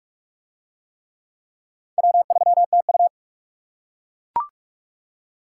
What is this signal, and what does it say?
Morse code, heard as a keyed tone, about 40 wpm, 700 Hz. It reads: W3TU